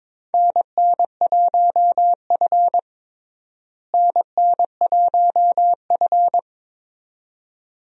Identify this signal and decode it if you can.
Morse code, heard as a keyed tone, about 22 wpm, 700 Hz. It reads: NN1F NN1F
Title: USA Call Signs - 4 Characters - Repeated 2x - 22wpm